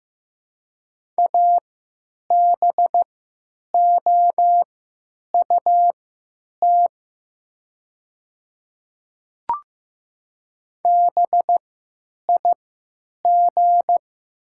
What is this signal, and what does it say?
Morse code, heard as a keyed tone, about 15 wpm, spaced as for 9 wpm, 700 Hz. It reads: ABOUT BIG
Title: 3x Character Spacing - Top 200 Words - 15wpm